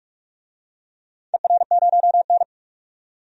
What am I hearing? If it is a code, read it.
Morse code, heard as a keyed tone, about 42 wpm, 700 Hz. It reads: ER0N
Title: DXCC-100 Call Signs - 4 Characters - 45wpm